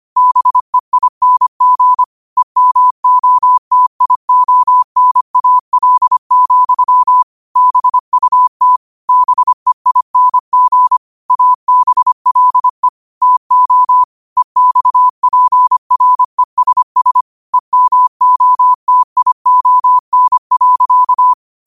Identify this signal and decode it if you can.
Morse code, heard as a keyed tone, about 25 wpm, 1000 Hz. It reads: BEING EMOTIONAL, BUT BEING ABLE TO EXPRESS EMOTION.
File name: qotdfiles / 25wpm